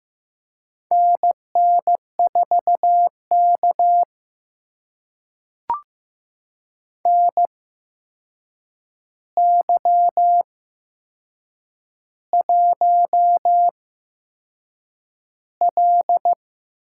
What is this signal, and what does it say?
Morse code, heard as a keyed tone, about 15 wpm, spaced as for 6 wpm, 700 Hz. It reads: NN4K NY1L
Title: USA Call Signs - 4 Characters - 8x Character Spacing - 15wpm